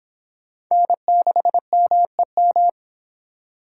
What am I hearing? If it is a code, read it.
Morse code, heard as a keyed tone, about 25 wpm, 700 Hz. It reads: N6MEM